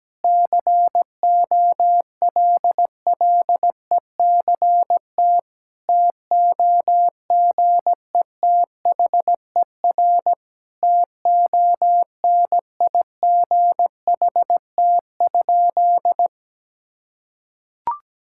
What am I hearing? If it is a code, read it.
Morse code, heard as a keyed tone, about 17 wpm, 700 Hz. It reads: COLLECT TOGETHER TONIGHT?
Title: Sentences from Top 1600 Words 17wpm